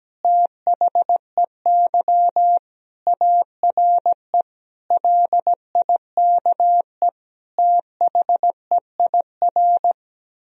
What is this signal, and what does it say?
Morse code, heard as a keyed tone, about 17 wpm, 700 Hz. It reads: THEY ARE LIKE THEIR
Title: Sentences from Top 1000 Words - Review Part 3 - 17wpm